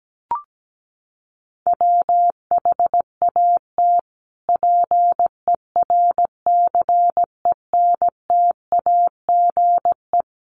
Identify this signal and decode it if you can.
Morse code, heard as a keyed tone, about 17 wpm, 700 Hz. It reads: WHAT PERCENTAGE
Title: Sentences from Top 1300 Words 17wpm